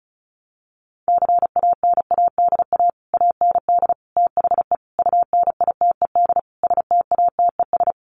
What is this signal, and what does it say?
Morse code, heard as a keyed tone, about 34 wpm, 700 Hz. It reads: CANADA AND THE UNITED STATES
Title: Sets of 5 Words Encore 35wpm